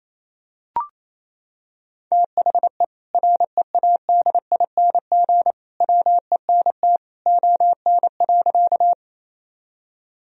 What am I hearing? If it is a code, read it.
Morse code, heard as a keyed tone, about 28 wpm, 700 Hz. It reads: THE READING WENT ON.
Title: Sentences from Top 1000 Words - Review Part 3 - 28wpm